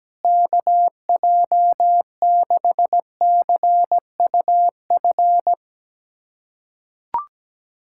Morse code, heard as a keyed tone, about 17 wpm, 700 Hz. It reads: KJ6CUF
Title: USA Callsign DE Callsign - 6 Characters - 17wpm